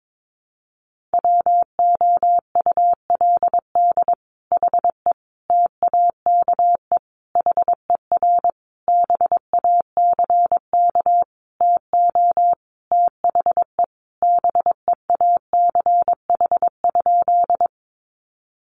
Morse code, heard as a keyed tone, about 22 wpm, 700 Hz. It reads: WOULD HE TAKE HER BACK TO THE BEACH?
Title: Sentences from Top 1500 Words 22wpm